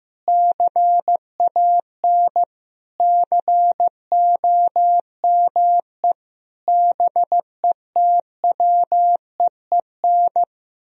Morse code, heard as a keyed tone, about 15 wpm, 700 Hz. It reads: CAN COME BETWEEN
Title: Sentences from Top 1000 Words - Review Part 1 - 15wpm